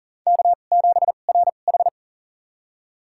Morse code, heard as a keyed tone, about 37 wpm, spaced as for 28 wpm, 700 Hz. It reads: K7RH
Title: USA Call Signs - 4 Characters - 2x Character Spacing - 40wpm